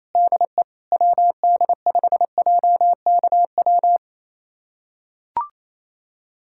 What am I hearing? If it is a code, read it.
Morse code, heard as a keyed tone, about 28 wpm, 700 Hz. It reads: DE WD5JKW
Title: USA Callsign DE Callsign - 6 Characters - 28wpm